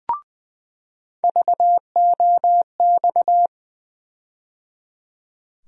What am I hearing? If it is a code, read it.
Morse code, heard as a keyed tone, about 21 wpm, 700 Hz. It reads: VOX
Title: USA Call Sign Postfixes - 3 Letters 20wpm